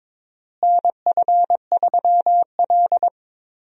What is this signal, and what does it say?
Morse code, heard as a keyed tone, about 22 wpm, 700 Hz. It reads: NF3L
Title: USA Call Signs - 4 Characters - Repeated 2x - 22wpm